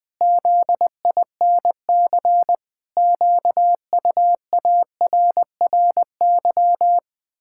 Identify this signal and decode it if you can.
Morse code, heard as a keyed tone, about 21 wpm, 700 Hz. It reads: ZINC QUARRY
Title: Pangram Challenge v2 - 20wpm